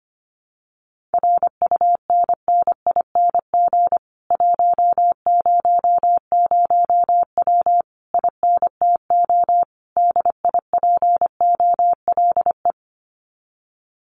Morse code, heard as a keyed tone, about 25 wpm, 700 Hz. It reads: RUNNING 100W INTO DIPOLE
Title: Common QSO Phrases Encore 25wpm